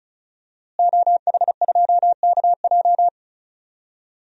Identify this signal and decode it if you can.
Morse code, heard as a keyed tone, about 34 wpm, 700 Hz. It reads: OH2KJ
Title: DXCC-100 Call Signs - 5 Characters - Repeated 2x - 35wpm